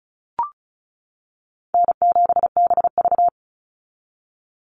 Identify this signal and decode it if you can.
Morse code, heard as a keyed tone, about 34 wpm, 700 Hz. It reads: N7BV
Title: Call Signs - 35wpm